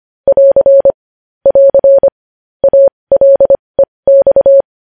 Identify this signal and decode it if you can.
Morse code, heard as a keyed tone, about 25 wpm, spaced as for 20 wpm, 550 Hz. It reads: <AR> <AR> ALEX
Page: news-headlines-in-morse-code-at-20-wpm